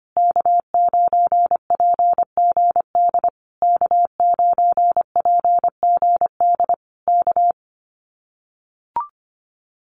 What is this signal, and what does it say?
Morse code, heard as a keyed tone, about 25 wpm, 700 Hz. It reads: K9PGD K9PGD K
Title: Common QSO Phrases Encore 25wpm